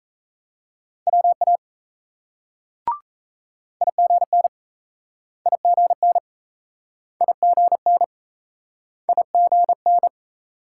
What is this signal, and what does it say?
Morse code, heard as a keyed tone, about 35 wpm, 700 Hz. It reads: WA IGN IGN IGN IGN